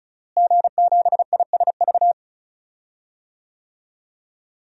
Morse code, heard as a keyed tone, about 36 wpm, 700 Hz. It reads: G7ISV
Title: DXCC-100 Call Signs - 5 Characters - Repeated 2x - 35wpm